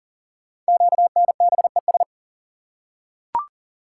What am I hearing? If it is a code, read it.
Morse code, heard as a keyed tone, about 39 wpm, 690 Hz. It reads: QNBES
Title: Code Groups - 5 Letters - 40wpm